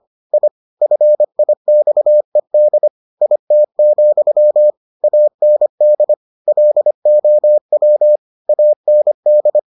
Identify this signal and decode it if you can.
Morse code, heard as a keyed tone, about 25 wpm, 600 Hz. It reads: I FIXED IT, AND LOW AND